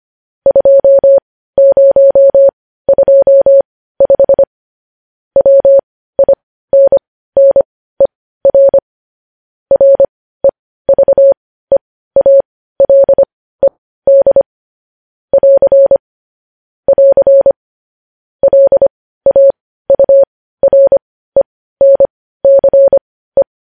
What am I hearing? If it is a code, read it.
Morse code, heard as a keyed tone, about 25 wpm, spaced as for 15 wpm, 550 Hz. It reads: 2025 WINNER REVEALED <AR> <AR> LAURENCE
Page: News headlines at 15